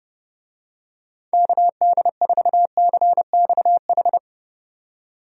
Morse code, heard as a keyed tone, about 30 wpm, 700 Hz. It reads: KD4CXH
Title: USA Call Signs - 6 Characters - 30wpm